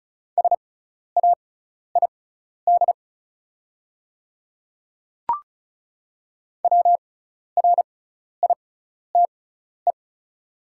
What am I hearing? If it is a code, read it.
Morse code, heard as a keyed tone, about 36 wpm, spaced as for 12 wpm, 690 Hz. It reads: SAID WRITE
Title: 6x Character Spacing - Top 100 Words - 35wpm